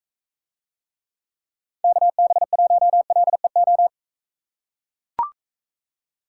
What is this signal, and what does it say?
Morse code, heard as a keyed tone, about 41 wpm, 700 Hz. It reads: KB1LEO